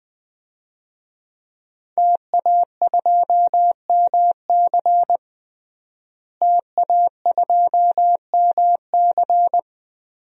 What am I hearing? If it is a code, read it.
Morse code, heard as a keyed tone, about 20 wpm, 700 Hz. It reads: TA2MC TA2MC
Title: DXCC-100 Call Signs - 5 Characters - Repeated 3x - 20wpm